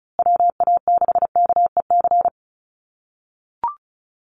Morse code, heard as a keyed tone, about 34 wpm, 700 Hz. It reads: WA6KEC
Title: USA Call Signs Repeated 2x 35wpm